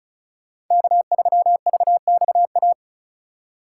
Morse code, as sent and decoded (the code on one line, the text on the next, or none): -.- ...-- ...- -..- .-
K3VXA